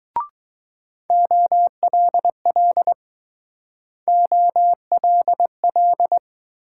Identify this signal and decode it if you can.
Morse code, heard as a keyed tone, about 20 wpm, 700 Hz. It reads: OLL OLL